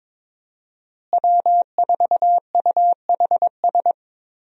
Morse code, heard as a keyed tone, about 22 wpm, 700 Hz. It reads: W4UHS
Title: USA Call Signs - 5 Characters - Repeated 2x - 22wpm